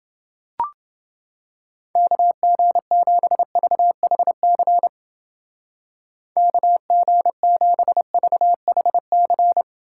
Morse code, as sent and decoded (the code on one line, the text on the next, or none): -.- --. --... ...- .... -.-. / -.- --. --... ...- .... -.-.
KG7VHC KG7VHC